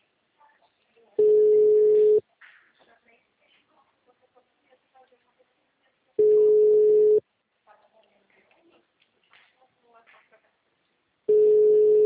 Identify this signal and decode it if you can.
Ringback tone (Europe, 425 Hz), with speech.
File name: out- -2034- - - .467